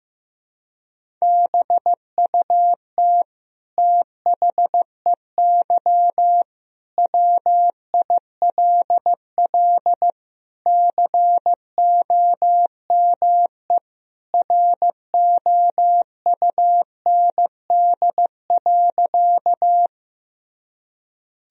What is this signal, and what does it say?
Morse code, heard as a keyed tone, about 15 wpm, 700 Hz. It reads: BUT THEY WILL COME ROUND.
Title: Sentences from Top 1000 Words - Review Part 1 - 15wpm